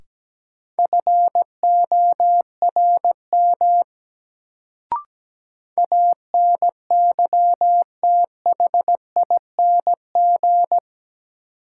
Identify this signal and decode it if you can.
Morse code, heard as a keyed tone, about 17 wpm, 700 Hz. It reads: FORM ANYTHING
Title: Sets of 4 Words - New Word Drill - 17wpm